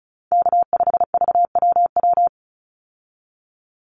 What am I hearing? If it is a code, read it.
Morse code, heard as a keyed tone, about 34 wpm, 700 Hz. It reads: K5VWW
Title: Call Signs Repeated 2x - 35wpm